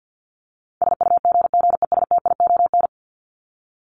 Morse code, heard as a keyed tone, about 48 wpm, 700 Hz. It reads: SUGGESTION
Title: Top 1501-1750 Words 50wpm